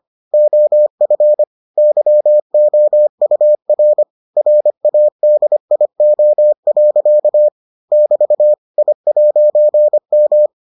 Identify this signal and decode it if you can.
Morse code, heard as a keyed tone, about 25 wpm, 600 Hz. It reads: OF YOUR RADIO. <BT> I'M